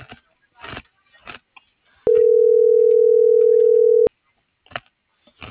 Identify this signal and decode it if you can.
Ringback tone (North America), 440 + 480 Hz.